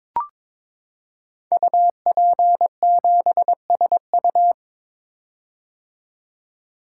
Morse code, heard as a keyed tone, about 22 wpm, 700 Hz. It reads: UP7SU
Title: Code Groups - 5 Letter:Numbers - 22wpm